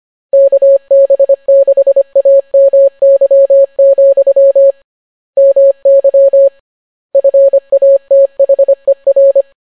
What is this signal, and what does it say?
Morse code, heard as a keyed tone, about 25 wpm, 550 Hz. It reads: KB6AMY, MY FATHER